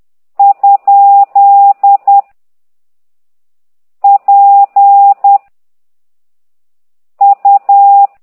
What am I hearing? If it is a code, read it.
Morse code, heard as a keyed tone, about 10 wpm, 800 Hz. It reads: ? P U
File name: ct2idw / exerc8